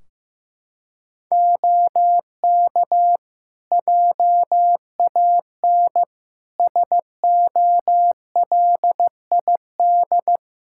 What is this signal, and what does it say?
Morse code, heard as a keyed tone, about 15 wpm, 700 Hz. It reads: OK JAN SOLID
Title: K1USN SST QSO Practice 15wpm